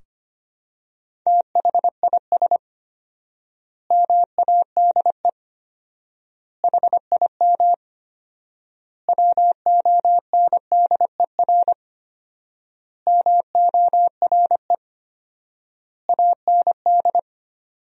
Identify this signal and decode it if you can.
Morse code, heard as a keyed tone, about 25 wpm, 700 Hz. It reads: THIS MADE HIM WONDER MORE AND
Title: 4x Word-Spacing - Sentences from Top 500 Words - 25wpm